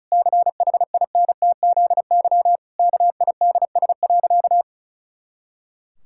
Morse code, heard as a keyed tone, about 36 wpm, 700 Hz. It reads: CHINTZY KIDS.
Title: Pangram Challenge v2 - 35wpm